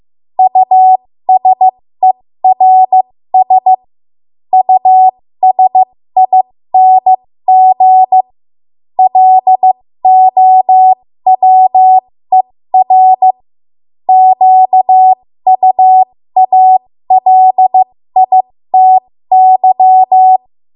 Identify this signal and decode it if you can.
Morse code, heard as a keyed tone, about 15 wpm, spaced as for 13 wpm, 750 Hz. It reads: USERS USING LOWER QUALITY
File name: Morse / 13WPM